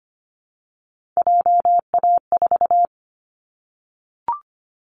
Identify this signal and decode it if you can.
Morse code, heard as a keyed tone, about 25 wpm, 700 Hz. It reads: JA4
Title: Callsign Prefixes for Top 25 DXCC - 25wpm